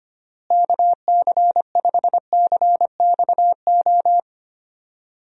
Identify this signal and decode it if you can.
Morse code, heard as a keyed tone, about 25 wpm, 700 Hz. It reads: KC5CXO
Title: USA Call Signs - 6 Characters - Repeated 2x - 25wpm